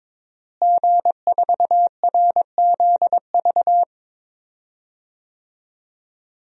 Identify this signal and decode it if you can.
Morse code, heard as a keyed tone, about 22 wpm, 700 Hz. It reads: G4RZV
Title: DXCC-100 Call Signs - 5 Characters - Repeated 3x - 22wpm